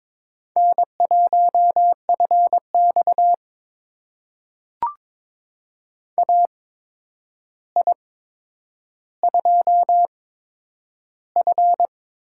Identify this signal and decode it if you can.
Morse code, heard as a keyed tone, about 23 wpm, spaced as for 10 wpm, 700 Hz. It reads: N1FX AI2F
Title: USA Call Signs - 4 Characters - 8x Character Spacing - 22wpm